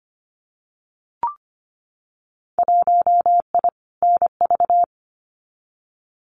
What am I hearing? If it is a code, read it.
Morse code, heard as a keyed tone, about 25 wpm, 700 Hz. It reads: 1I NV
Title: Winter Field Day Exchanges Encore 25wpm